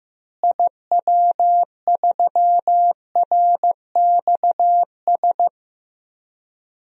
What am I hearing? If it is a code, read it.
Morse code, heard as a keyed tone, about 15 wpm, 700 Hz. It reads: IW3RXS
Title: DXCC-100 Call Signs - 6 Characters - Repeated 3x - 15wpm